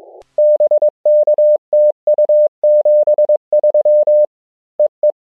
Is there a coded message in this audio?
Yes: Morse code from 0.4 to 5.1 s, heard as a keyed tone, about 22 wpm, 600 Hz. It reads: BK TU 73 EE